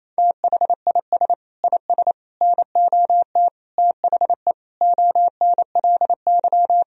Morse code, heard as a keyed tone, about 28 wpm, 700 Hz. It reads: THIS IS NOT THE ONLY
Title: Sentences from Top 1000 Words - Review Part 2 - 28wpm